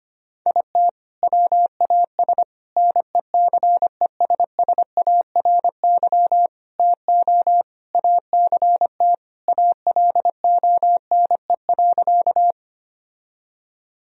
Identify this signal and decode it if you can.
Morse code, heard as a keyed tone, about 25 wpm, 700 Hz. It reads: IT WAS NECESSARY TO ACT ALONE.
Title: Sentences from Top 1000 Words - Review Part 1 - 25wpm